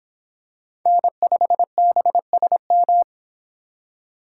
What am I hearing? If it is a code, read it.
Morse code, heard as a keyed tone, about 23 wpm, 700 Hz. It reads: N5BSM